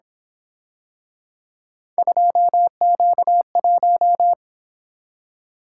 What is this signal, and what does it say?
Morse code, heard as a keyed tone, about 26 wpm, 700 Hz. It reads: 2Q1
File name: Speed-Racing - Three Letter-Number Combinations - 22wpm